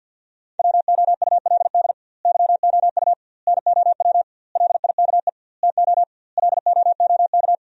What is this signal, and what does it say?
Morse code, heard as a keyed tone, about 50 wpm, 700 Hz. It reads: WOULD YOU NOW LIKE TO LOOK
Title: 2x Word-Spacing - Sentences from Top 200 Words - 50wpm